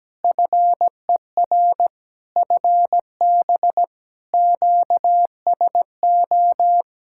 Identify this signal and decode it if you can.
Morse code, heard as a keyed tone, about 17 wpm, 700 Hz. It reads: FER FB QSO
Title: Practice QSO - Exchange 4 - Ending - 17wpm